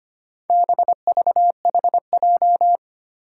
Morse code, heard as a keyed tone, about 25 wpm, 700 Hz. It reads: BVHJ
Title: Four Challenging Letter-Number Combinations 25wpm